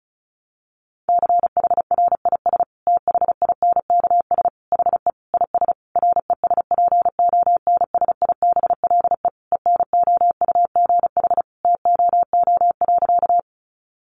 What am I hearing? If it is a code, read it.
Morse code, heard as a keyed tone, about 34 wpm, 700 Hz. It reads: CHRIS THINKS HE IS RESPONSIBLE ENOUGH TOO.
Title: Easy Phrases 35wpm